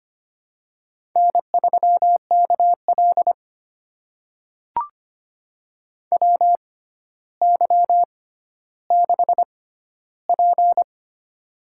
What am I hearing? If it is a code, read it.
Morse code, heard as a keyed tone, about 25 wpm, spaced as for 13 wpm, 700 Hz. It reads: N3KL WY6P
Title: USA Call Signs - 4 Characters - 6x Character Spacing - 25wpm